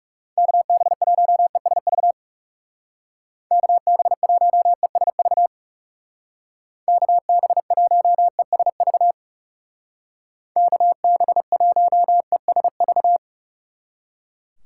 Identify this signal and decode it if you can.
Morse code, heard as a keyed tone, about 37 wpm, 700 Hz. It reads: KB1ESV KB1ESV KB1ESV KB1ESV
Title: Speed Racing - USA 2x3 Call Signs 30wpm